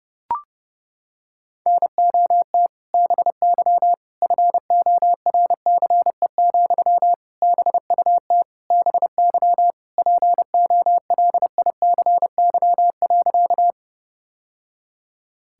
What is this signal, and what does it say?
Morse code, heard as a keyed tone, about 32 wpm, 700 Hz. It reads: NOT BY FORCE, BUT BY POLICY.
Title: Sentences from Top 1000 Words - Review Part 5 - 30wpm